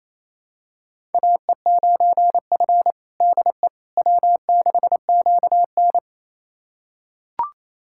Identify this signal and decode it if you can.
Morse code, heard as a keyed tone, about 28 wpm, 700 Hz. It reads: AE9F DE W6QN
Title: USA Callsign DE Callsign - 4 Characters - 28wpm